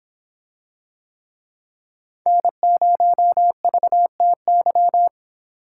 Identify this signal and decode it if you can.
Morse code, heard as a keyed tone, about 25 wpm, 700 Hz. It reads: N0VTY